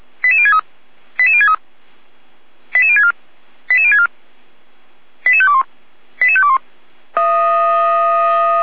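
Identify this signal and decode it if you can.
Five-tone selective call (ZVEI), heard as quick runs of five tones, tones of about 80 ms. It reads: 88063 88063 88064 88064 88521 88521
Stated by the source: Five-tone ZVEI selective call 88247 received by SDR, fire brigade Feuerwehr Orlamünde.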